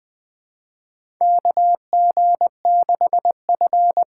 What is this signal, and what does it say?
Morse code, heard as a keyed tone, about 20 wpm, 700 Hz. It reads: KG6F
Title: USA Call Signs - 6 Characters - 6x Character Spacing - 20wpm